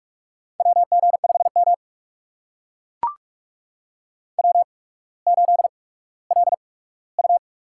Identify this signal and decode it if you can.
Morse code, heard as a keyed tone, about 44 wpm, spaced as for 19 wpm, 700 Hz. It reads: WG5M W8LU
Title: USA Call Signs - 4 Characters - 8x Character Spacing - 45wpm